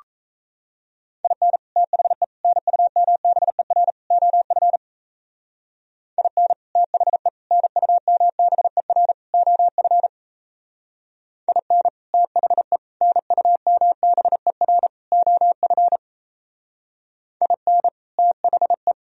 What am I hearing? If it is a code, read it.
Morse code, heard as a keyed tone, about 38 wpm, 700 Hz. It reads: IN THE NUMBER OF IN THE NUMBER OF IN THE NUMBER OF IN THE